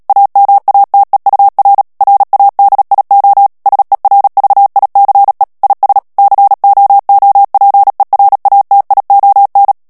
Morse code, heard as a keyed tone, about 37 wpm, 800 Hz. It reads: AMATEUR RADIO SERVICE IS COOPERATION
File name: morse mp3 / ANGLAIS texte clair 3 vitesse 35 mots minute